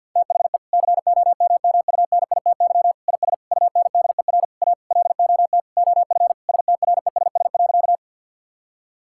Morse code, heard as a keyed tone, about 50 wpm, 700 Hz. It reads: THE COMMUNITY IS UNDER A LOT OF STRESS.
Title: Sentences from Top 1000 Words - Review Part 6 - 50wpm